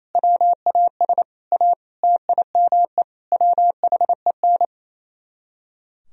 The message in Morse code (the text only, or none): WAS A TIME WHEN